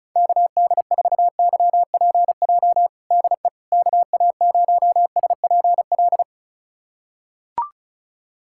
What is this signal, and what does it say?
Morse code, heard as a keyed tone, about 36 wpm, 700 Hz. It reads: KD4YPJ DE KA0SPL
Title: USA Callsign DE Callsign - 6 Characters - 35wpm